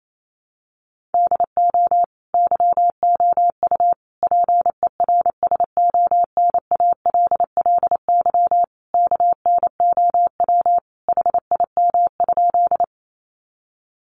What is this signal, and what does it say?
Morse code, heard as a keyed tone, about 28 wpm, 700 Hz. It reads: DO YOU PERSONALLY KNOW HIM?
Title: Sentences from Top 1400 Words 28wpm